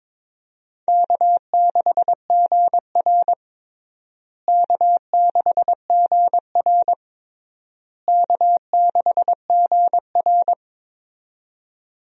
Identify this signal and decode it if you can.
Morse code, heard as a keyed tone, about 22 wpm, 700 Hz. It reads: K6GR K6GR K6GR
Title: USA Call Signs - 4 Characters - Repeated 3x - 22wpm